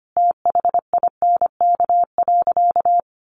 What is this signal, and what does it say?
Morse code, heard as a keyed tone, about 25 wpm, 700 Hz. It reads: THINK.
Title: Sentences from Top 600 Words 25wpm